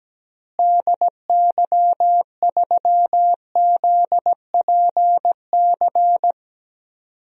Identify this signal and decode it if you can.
Morse code, heard as a keyed tone, about 17 wpm, 700 Hz. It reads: DY3ZPC
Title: DXCC-100 Call Signs - 6 Characters - Repeated 3x - 17wpm